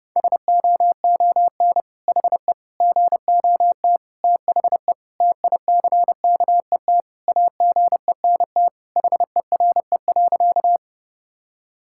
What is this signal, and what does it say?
Morse code, heard as a keyed tone, about 32 wpm, 700 Hz. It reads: SOON HE GOT THE TICKET AGENT HERE.
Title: Sentences from Top 1600 Words 30wpm